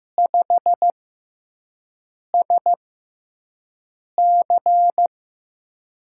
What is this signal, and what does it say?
Morse code, heard as a keyed tone, about 15 wpm, spaced as for 5 wpm, 700 Hz. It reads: 5SC
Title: USA Call Signs - 4 Characters - 6x Character Spacing - 15wpm